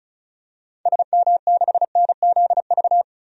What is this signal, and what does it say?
Morse code, heard as a keyed tone, about 36 wpm, 690 Hz. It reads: SM6NZV
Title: DXCC-100 Call Signs - 6 Characters - 35wpm